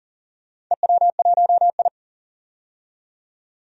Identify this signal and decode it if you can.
Morse code, heard as a keyed tone, about 38 wpm, 700 Hz. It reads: EW1I
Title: DXCC-100 Call Signs - 4 Characters - 40wpm